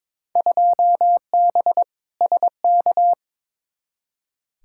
Morse code, heard as a keyed tone, about 22 wpm, 700 Hz. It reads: 2B SK
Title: Field Day Exchanges Encore - 22wpm